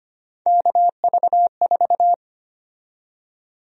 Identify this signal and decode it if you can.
Morse code, heard as a keyed tone, about 25 wpm, 700 Hz. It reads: KV4
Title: USA Call Sign Prefixes - Repeated 2x - 25wpm